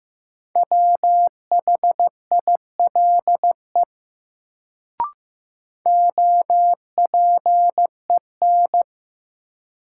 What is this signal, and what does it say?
Morse code, heard as a keyed tone, about 15 wpm, 700 Hz. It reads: WHILE OPEN
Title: Sentences from Top 300 Words - New Word Drill - 15wpm